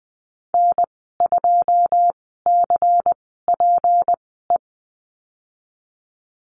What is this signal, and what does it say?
Morse code, heard as a keyed tone, about 20 wpm, spaced as for 15 wpm, 700 Hz. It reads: N2CPE
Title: 161 Lesson 26g - Letter g - Call Signs 20wpm